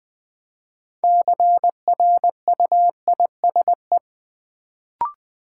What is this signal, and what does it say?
Morse code, heard as a keyed tone, about 20 wpm, 700 Hz. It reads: CRUISE
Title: Build A Word - 2 Letters At A Time - 6-Letter Words - 20wpm